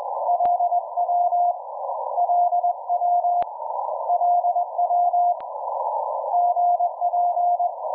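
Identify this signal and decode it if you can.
Morse code, heard as a keyed tone, about 21 wpm, 710 Hz. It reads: LW LW LW GP
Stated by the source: WAKE Up! QRP Sprint contest QSO with UI7K, who switched between vertical and L-shaped antennas; poor propagation with rapidly changing signal levels.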